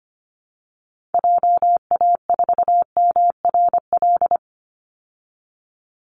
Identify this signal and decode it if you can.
Morse code, heard as a keyed tone, about 25 wpm, 700 Hz. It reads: JA4MRL
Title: Call Signs Repeated 3x - 25wpm